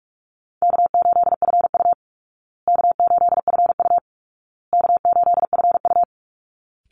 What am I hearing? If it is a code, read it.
Morse code, heard as a keyed tone, about 44 wpm, 700 Hz. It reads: K8FU K8FU K8FU
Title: Easy Call Signs Repeated 3x - 45wpm